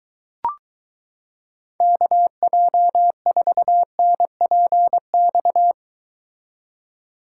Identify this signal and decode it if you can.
Morse code, heard as a keyed tone, about 23 wpm, 700 Hz. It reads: KJ4NPX